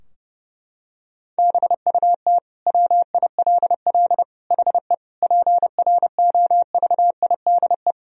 Morse code, heard as a keyed tone, about 30 wpm, 700 Hz. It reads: BUT WILL HE PROVIDE
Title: Sentences from Top 1000 Words - Review Part 3 - 30wpm